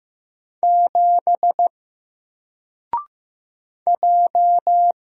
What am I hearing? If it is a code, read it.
Morse code, heard as a keyed tone, about 15 wpm, 700 Hz. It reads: 7 J
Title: Single Letter-Number 15wpm